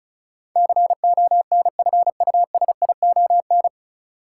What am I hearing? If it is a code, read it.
Morse code, heard as a keyed tone, about 36 wpm, 700 Hz. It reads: CONFUSION
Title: Sentences from Top 1600 Words - New Word Drill - 35wpm